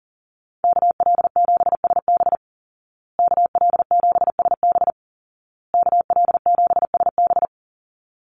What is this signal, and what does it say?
Morse code, heard as a keyed tone, about 40 wpm, 700 Hz. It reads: KL7SB KL7SB KL7SB
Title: Call Signs Repeated 2x - 40wpm